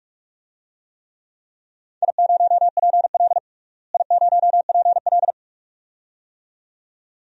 Morse code, heard as a keyed tone, about 41 wpm, 700 Hz. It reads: I0PL I0PL
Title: DXCC-100 Call Signs - 4 Characters - Repeated 2x - 45wpm